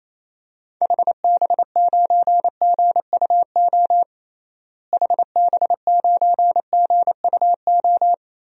Morse code, heard as a keyed tone, about 28 wpm, 700 Hz. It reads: HB9GUO HB9GUO
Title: DXCC-100 Call Signs - 6 Characters - Repeated 3x - 28wpm